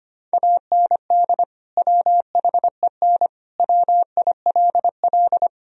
Morse code, heard as a keyed tone, about 25 wpm, 700 Hz. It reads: AND WHEN WILL
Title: Sentences from Top 1000 Words - Review Part 4 - 25wpm